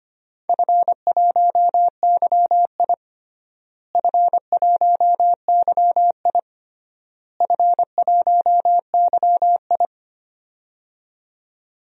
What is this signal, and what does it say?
Morse code, heard as a keyed tone, about 25 wpm, 700 Hz. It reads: F1YI F1YI F1YI
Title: DXCC-100 Call Signs - 4 Characters - Repeated 3x - 25wpm